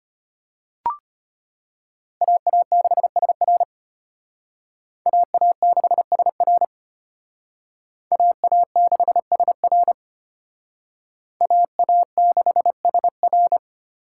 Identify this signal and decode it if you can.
Morse code, heard as a keyed tone, about 33 wpm, 700 Hz. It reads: AA6SR AA6SR AA6SR AA6SR